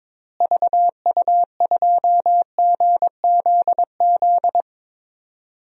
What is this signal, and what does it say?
Morse code, heard as a keyed tone, about 22 wpm, 700 Hz. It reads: VU2GZZ
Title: DXCC-100 Call Signs - 6 Characters - Repeated 3x - 22wpm